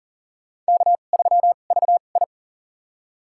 Morse code, heard as a keyed tone, about 40 wpm, spaced as for 29 wpm, 700 Hz. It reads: K3VI
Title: USA Call Signs - 4 Characters - 2x Character Spacing - 40wpm